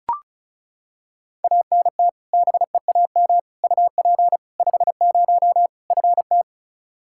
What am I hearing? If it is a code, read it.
Morse code, heard as a keyed tone, about 34 wpm, 700 Hz. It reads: ANT BEAM UP 50 FT
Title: Practice QSO - Exchange 2 - Info - 35wpm